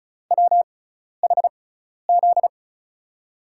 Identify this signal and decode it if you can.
Morse code, heard as a keyed tone, about 36 wpm, spaced as for 12 wpm, 700 Hz. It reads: WHZ
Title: USA Call Signs - 5 Characters - 6x Character Spacing - 35wpm